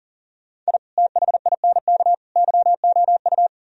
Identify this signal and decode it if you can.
Morse code, heard as a keyed tone, about 38 wpm, 700 Hz. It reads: I THINK YOU
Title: Speed-Racing - Sets of 3 Words 40wpm